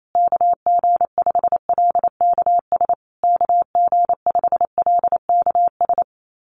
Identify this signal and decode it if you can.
Morse code, heard as a keyed tone, about 28 wpm, 700 Hz. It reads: KG5LKS KG5LKS
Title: USA Call Signs Repeated 3x 28wpm